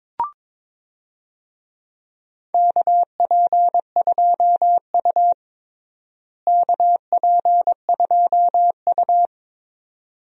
Morse code, heard as a keyed tone, about 22 wpm, 700 Hz. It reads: KP2U KP2U
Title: DXCC-100 Call Signs - 4 Characters - Repeated 2x - 22wpm